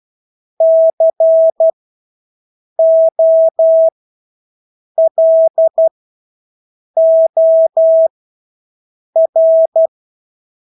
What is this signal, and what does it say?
Morse code, heard as a keyed tone, about 12 wpm, spaced as for 6 wpm, 650 Hz. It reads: COLOR
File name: Engelske ord 12-6 wpm